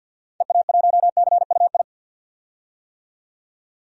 Morse code, heard as a keyed tone, about 47 wpm, 700 Hz. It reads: EA1CUI
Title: DXCC-100 Call Signs - 6 Characters - 50wpm